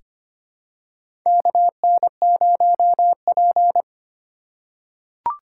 Morse code, heard as a keyed tone, about 25 wpm, 700 Hz. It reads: KN0P
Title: USA Call Signs - 4 Characters - 4x Character Spacing - 25wpm